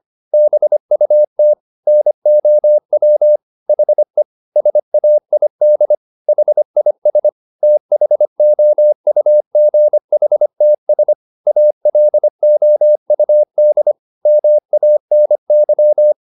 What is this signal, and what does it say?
Morse code, heard as a keyed tone, about 25 wpm, 600 Hz. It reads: BUT NOW HE SAID HIS THOUGHTS ALOUD MANY
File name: the old man and the sea part 1 25wpm